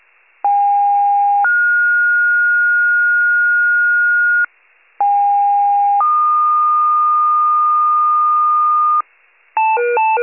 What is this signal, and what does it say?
Two-tone page, heard as a one-second tone, then a three-second tone, about 810 then 1430 Hz, about 810 then 1220 Hz, with noise. It is heard from 0.4 to 9.0 s.
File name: wcalert